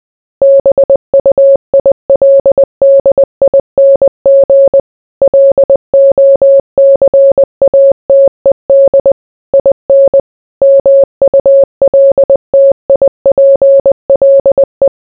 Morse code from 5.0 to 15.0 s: LOCATED IN MULTIPLE